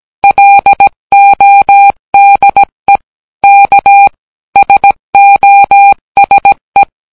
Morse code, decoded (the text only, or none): LODE K SOSE